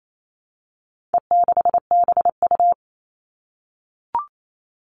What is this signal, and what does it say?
Morse code, heard as a keyed tone, about 28 wpm, 700 Hz. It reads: E6BU
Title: All Cut Numbers - 4-Digit 28wpm